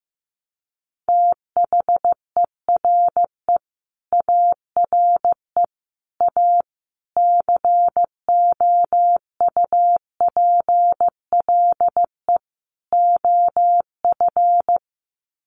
Morse code, heard as a keyed tone, about 15 wpm, 700 Hz. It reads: THERE ARE A COUPLE OF
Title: Sets of 5 Words Encore 15wpm